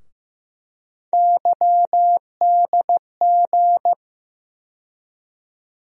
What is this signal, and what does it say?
Morse code, heard as a keyed tone, about 15 wpm, 700 Hz. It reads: YDG
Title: USA Call Sign Postfixes - 3 Letters 15wpm